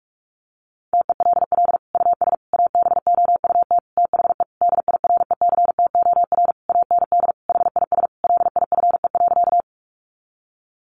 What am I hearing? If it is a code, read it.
Morse code, heard as a keyed tone, about 44 wpm, 700 Hz. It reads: TELL US ABOUT THE DIRECTOR AND HIS LIFE.
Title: Sentences from Top 1300 Words 45wpm